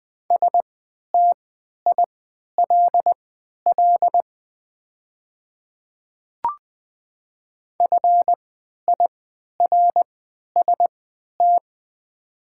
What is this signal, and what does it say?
Morse code, heard as a keyed tone, about 20 wpm, spaced as for 11 wpm, 700 Hz. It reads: STILL FIRST
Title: 3x Character Spacing - Top 300 Words - 20wpm